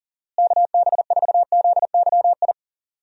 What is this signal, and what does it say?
Morse code, heard as a keyed tone, about 40 wpm, 700 Hz. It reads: KB4ZYI
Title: USA Call Signs - 6 Characters - Repeated 2x - 40wpm